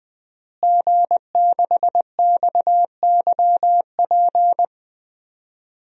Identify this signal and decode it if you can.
Morse code, heard as a keyed tone, about 20 wpm, 700 Hz. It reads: G6XYP
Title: DXCC-100 Call Signs - 5 Characters - Repeated 3x - 20wpm